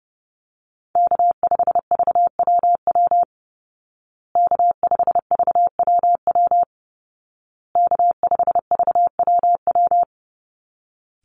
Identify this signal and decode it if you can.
Morse code, heard as a keyed tone, about 30 wpm, 700 Hz. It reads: K5VWW K5VWW K5VWW
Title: Call Signs Repeated 2x - 30wpm